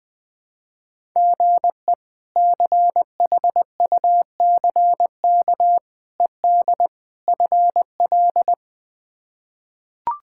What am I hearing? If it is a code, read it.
Morse code, heard as a keyed tone, about 20 wpm, 700 Hz. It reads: GE CHUCK ED FL
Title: K1USN SST QSO Practice - 20wpm